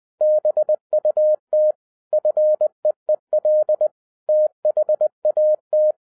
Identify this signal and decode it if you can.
Morse code, heard as a keyed tone, about 20 wpm, 610 Hz. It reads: BUT FEEL THAT
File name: 2011-07-mp3 / n08 A LOT HAS CHANGED 20wpm